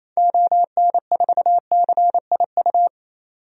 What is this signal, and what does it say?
Morse code, heard as a keyed tone, about 28 wpm, 700 Hz. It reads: ON4CIU
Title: DXCC-100 Call Signs - 6 Characters - 28wpm